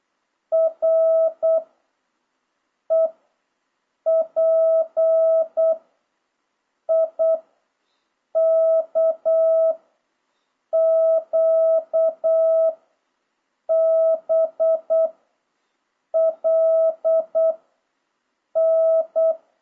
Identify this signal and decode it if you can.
Morse code, heard as a keyed tone, about 8 wpm, 660 Hz. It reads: R E P I K Q B L N
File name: jh8cbh / 2026cw